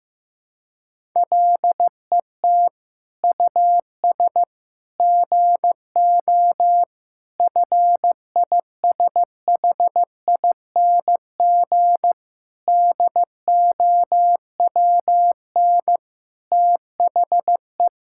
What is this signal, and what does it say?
Morse code, heard as a keyed tone, about 16 wpm, 700 Hz. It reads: LET US GO FISHING DOWN THE
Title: Sentences from Top 1000 Words - Review Part 5 - 15wpm